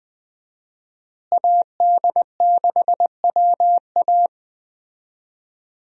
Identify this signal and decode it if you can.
Morse code, heard as a keyed tone, about 20 wpm, 700 Hz. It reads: AD6WA
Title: USA Call Signs - 5 Characters - Repeated 3x - 20wpm